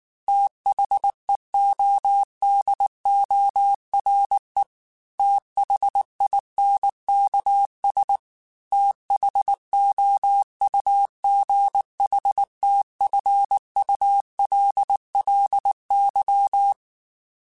Morse code, heard as a keyed tone, about 19 wpm, 800 Hz. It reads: THEODORE THINKS THOUGHTFULLY
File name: TH Runs / TH Runs mp3s / TH Runs @18